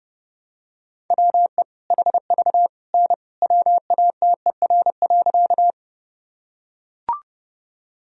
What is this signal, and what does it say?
Morse code, heard as a keyed tone, about 32 wpm, 700 Hz. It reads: WE HV N WATER.
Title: Sentences from Top 100 Words and Abbreviations 30wpm